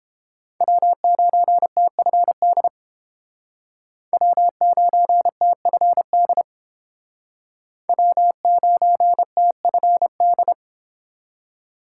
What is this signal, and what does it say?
Morse code, heard as a keyed tone, about 30 wpm, 700 Hz. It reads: W9TFD W9TFD W9TFD